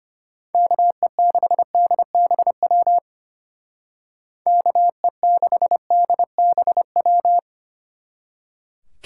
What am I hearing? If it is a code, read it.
Morse code, heard as a keyed tone, about 32 wpm, 700 Hz. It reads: KE6DBW KE6DBW